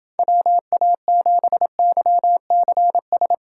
Morse code, heard as a keyed tone, about 29 wpm, 700 Hz. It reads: WA7YCS